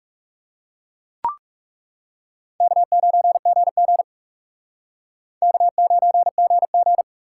Morse code, heard as a keyed tone, about 37 wpm, 700 Hz. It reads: K9GG K9GG
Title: Speed Racing - USA 1x2 Call Signs 30wpm